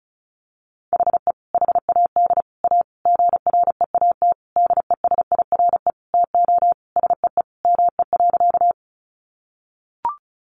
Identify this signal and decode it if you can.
Morse code, heard as a keyed tone, about 34 wpm, 700 Hz. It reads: HE HAD A GREAT DESIRE TO SEE ME.
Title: Sentences from Top 1100 Words 35wpm